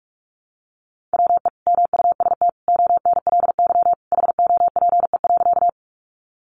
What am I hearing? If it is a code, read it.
Morse code, heard as a keyed tone, about 44 wpm, 700 Hz. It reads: WE MUST ONLY HOPE.
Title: Sentences from Top 1000 Words 45wpm